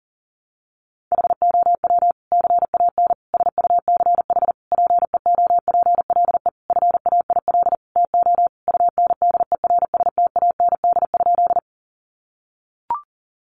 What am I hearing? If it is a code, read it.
Morse code, heard as a keyed tone, about 40 wpm, 700 Hz. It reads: HOW CAN SUCH PEOPLE FAIL TO UNDERSTAND?
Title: Sentences from Top 1100 Words 40wpm